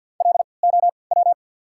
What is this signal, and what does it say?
Morse code, heard as a keyed tone, about 53 wpm, spaced as for 29 wpm, 690 Hz. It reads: LOW
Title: 3x Character Spacing - Top 300 Words - 50wpm